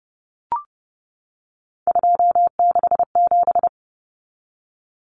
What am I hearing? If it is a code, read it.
Morse code, heard as a keyed tone, about 30 wpm, 700 Hz. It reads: 267
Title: Numbers - 3-Digit 30wpm